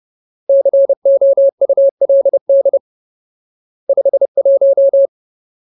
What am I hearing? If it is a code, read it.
Morse code, heard as a keyed tone, about 30 wpm, 550 Hz. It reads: COULD 51
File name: Ultimate Word List / 30WPM Ultimate